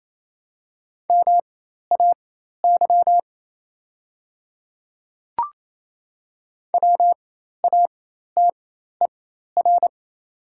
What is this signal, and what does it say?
Morse code, heard as a keyed tone, about 28 wpm, spaced as for 13 wpm, 700 Hz. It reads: MAY WATER
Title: 4x Character Spacing - Top 100 Words - 28wpm